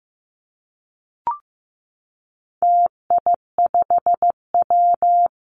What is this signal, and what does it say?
Morse code, heard as a keyed tone, about 15 wpm, 700 Hz. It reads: TI5W
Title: Call Signs Repeated 2x - 15wpm